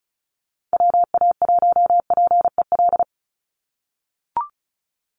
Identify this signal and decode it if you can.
Morse code, heard as a keyed tone, about 34 wpm, 700 Hz. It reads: WA1PEL
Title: USA Call Signs 35wpm